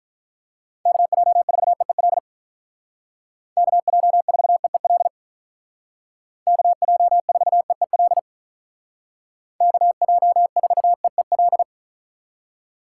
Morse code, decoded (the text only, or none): KJ4EEL KJ4EEL KJ4EEL KJ4EEL